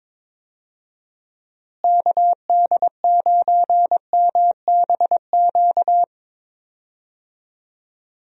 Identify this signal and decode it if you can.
Morse code, heard as a keyed tone, about 21 wpm, 700 Hz. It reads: KD9MBQ
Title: USA Call Signs - 6 Characters - Repeated 2x - 22wpm